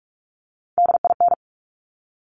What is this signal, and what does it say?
Morse code, heard as a keyed tone, about 46 wpm, 700 Hz. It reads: DIN
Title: Top Three Letter Combinations in Words 45wpm